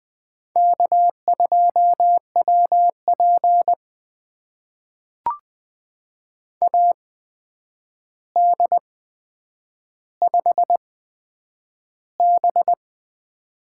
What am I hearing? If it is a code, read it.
Morse code, heard as a keyed tone, about 20 wpm, spaced as for 9 wpm, 700 Hz. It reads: K2WP AD5B
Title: USA Call Signs - 4 Characters - 8x Character Spacing - 20wpm